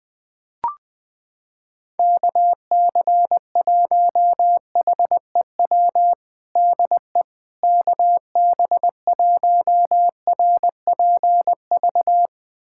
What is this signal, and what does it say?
Morse code, heard as a keyed tone, about 20 wpm, 700 Hz. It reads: KC1HEW DE KB1RPV
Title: USA Callsign DE Callsign - 6 Characters - 20wpm